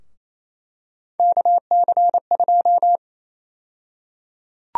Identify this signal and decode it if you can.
Morse code, heard as a keyed tone, about 28 wpm, 700 Hz. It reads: KC2
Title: USA Call Sign Prefixes - 28wpm